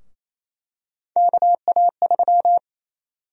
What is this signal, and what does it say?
Morse code, heard as a keyed tone, about 28 wpm, 700 Hz. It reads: KA3
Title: USA Call Sign Prefixes - 28wpm